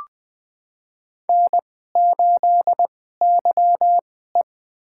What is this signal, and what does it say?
Morse code, heard as a keyed tone, about 20 wpm, spaced as for 15 wpm, 700 Hz. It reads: N8YE
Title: USA Call Signs - 2x Character Spacing - 20wpm